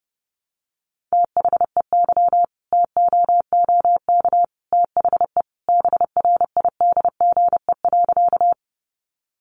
Morse code, heard as a keyed tone, about 30 wpm, 700 Hz. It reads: THEY TOOK THE BRIDGE.
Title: Sentences from Top 1200 Words 30wpm